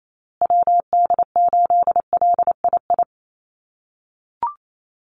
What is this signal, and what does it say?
Morse code, heard as a keyed tone, about 28 wpm, 700 Hz. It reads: WD8LII
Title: USA Call Signs Repeated 2x 28wpm